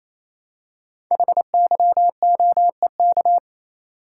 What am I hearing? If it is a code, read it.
Morse code, heard as a keyed tone, about 29 wpm, 700 Hz. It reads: HYOEK
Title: Code Groups - 5 Letters - 28wpm